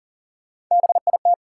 Morse code, heard as a keyed tone, about 40 wpm, 700 Hz. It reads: BIT
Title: Sets of 3 Words - New Word Drill - 40wpm